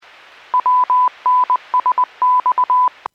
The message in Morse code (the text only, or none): WNSX